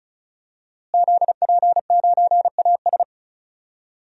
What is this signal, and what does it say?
Morse code, heard as a keyed tone, about 34 wpm, 700 Hz. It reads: ZP9AS
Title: DXCC-100 Call Signs - 5 Characters - Repeated 3x - 35wpm